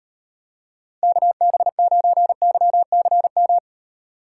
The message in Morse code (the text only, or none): KB9YCM